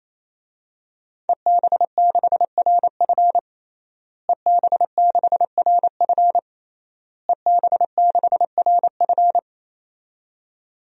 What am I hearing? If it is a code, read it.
Morse code, heard as a keyed tone, about 28 wpm, 700 Hz. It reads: EB6RF EB6RF EB6RF
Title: DXCC-100 Call Signs - 5 Characters - Repeated 3x - 28wpm